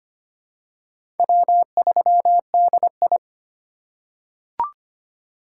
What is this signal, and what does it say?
Morse code, heard as a keyed tone, about 25 wpm, 700 Hz. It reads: W3DI
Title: USA Call Signs - 4 Characters - 4x Character Spacing - 25wpm